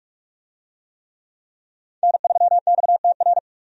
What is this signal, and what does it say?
Morse code, heard as a keyed tone, about 42 wpm, 700 Hz. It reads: N3XTR
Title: USA Call Signs - 5 Characters - 45wpm